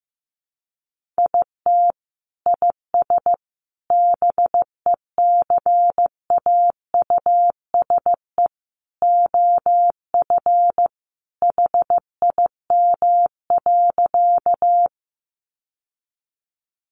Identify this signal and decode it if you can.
Morse code, heard as a keyed tone, about 15 wpm, 700 Hz. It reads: IT IS BECAUSE OF HIM.
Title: Sentences from Top 600 Words 15wpm